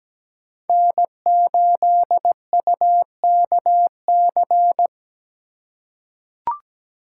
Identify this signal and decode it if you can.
Morse code, heard as a keyed tone, about 17 wpm, 700 Hz. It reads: N8UKC
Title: USA Call Signs - 5 Characters - 3x Character Spacing - 17wpm